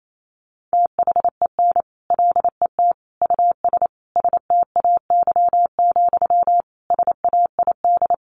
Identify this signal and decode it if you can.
Morse code, heard as a keyed tone, about 28 wpm, 700 Hz. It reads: THEN LET US STAY, SAID
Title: Sentences from Top 1500 Words 28wpm